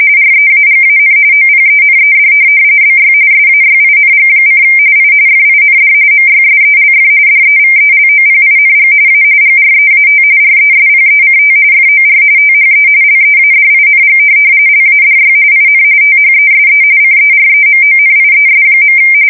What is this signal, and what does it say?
RTTY, about 45.45 baud. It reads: REAGAN OPENED BY SAYING: I FIND MYSELF DEVELOPING A GREAT RESPECT FOR THE LEADING LADY IN THESE CEREMONIES.
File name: fsk170 / USS-New-Jersey